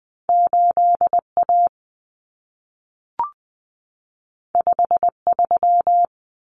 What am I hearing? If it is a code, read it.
Morse code, heard as a keyed tone, about 20 wpm, 700 Hz. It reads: 8 A 5 3
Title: Common Cut Numbers - 2-Digit 20wpm